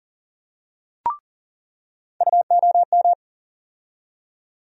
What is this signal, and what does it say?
Morse code, heard as a keyed tone, about 40 wpm, 700 Hz. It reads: UOM
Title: USA Call Sign Postfixes - 3 Letters 40wpm